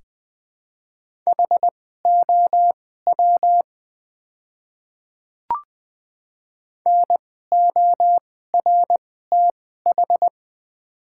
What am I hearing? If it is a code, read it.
Morse code, heard as a keyed tone, about 20 wpm, spaced as for 14 wpm, 700 Hz. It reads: HOW NORTH
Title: 2x Character Spacing - Top 300 Words - 20wpm